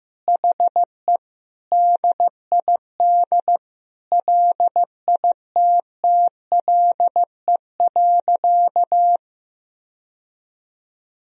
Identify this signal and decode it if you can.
Morse code, heard as a keyed tone, about 15 wpm, 700 Hz. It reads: HE DID LITTLE.
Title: Sentences from Top 1000 Words - Review Part 1 - 15wpm